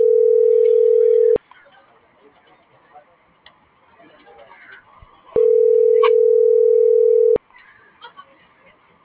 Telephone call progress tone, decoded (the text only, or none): ringback tone